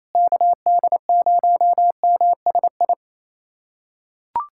Morse code, heard as a keyed tone, about 28 wpm, 700 Hz. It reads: KD0MSI
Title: USA Callsign DE Callsign - 6 Characters - 28wpm